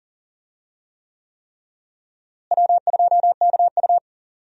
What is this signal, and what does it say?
Morse code, heard as a keyed tone, about 40 wpm, 700 Hz. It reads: W2KU
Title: USA Call Signs - 4 Characters - Repeated 3x - 40wpm